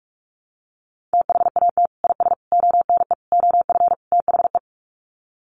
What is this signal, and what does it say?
Morse code, heard as a keyed tone, about 44 wpm, 700 Hz. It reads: THAT IS ONE OF THE
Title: Sets of 5 Words Encore 45wpm